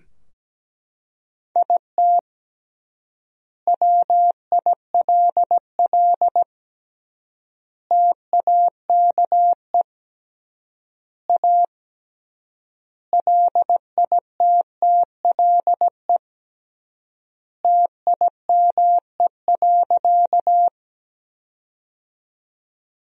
Morse code, heard as a keyed tone, about 17 wpm, 700 Hz. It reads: IT WILL TAKE A LITTLE TIME.
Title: 3x Word-Spacing - Sentences from Top 200 Words - 17wpm